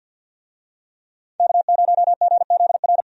Morse code, heard as a keyed tone, about 53 wpm, 700 Hz. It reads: K0GZR
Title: USA Call Signs - 5 Characters - 50wpm